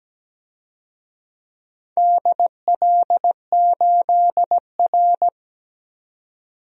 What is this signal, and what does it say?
Morse code, heard as a keyed tone, about 17 wpm, 700 Hz. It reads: DL8R
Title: DXCC-100 Call Signs - 4 Characters - Repeated 2x - 17wpm